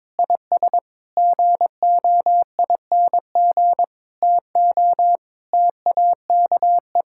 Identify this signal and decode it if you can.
Morse code, heard as a keyed tone, about 22 wpm, 700 Hz. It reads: IS GOING TO TAKE
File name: practice / 22wpm / Sentences with 3-Word Phrases - 22wpm